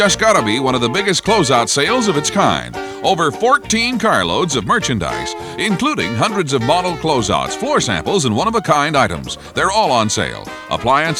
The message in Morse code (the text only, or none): none